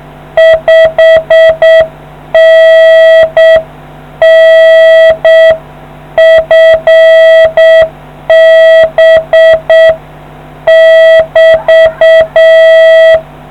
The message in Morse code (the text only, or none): 5NN FB <BT>